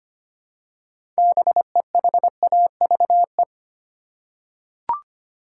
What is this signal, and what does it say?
Morse code, heard as a keyed tone, about 25 wpm, 700 Hz. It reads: BEHAVE
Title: Six Letter Words - 25wpm